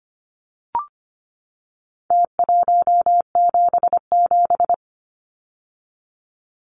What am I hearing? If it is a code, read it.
Morse code, heard as a keyed tone, about 25 wpm, 700 Hz. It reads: T177
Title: Numbers - Contest Style - 4-Digit 25wpm